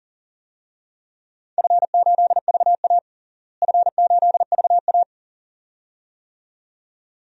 Morse code, heard as a keyed tone, about 40 wpm, 700 Hz. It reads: F8VA F8VA
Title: DXCC-100 Call Signs - 4 Characters - Repeated 2x - 40wpm